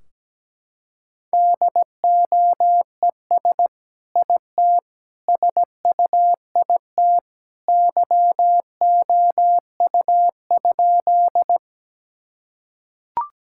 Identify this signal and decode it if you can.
Morse code, heard as a keyed tone, about 17 wpm, 700 Hz. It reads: DOES IT SUIT YOU?
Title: Sentences from Top 1600 Words 17wpm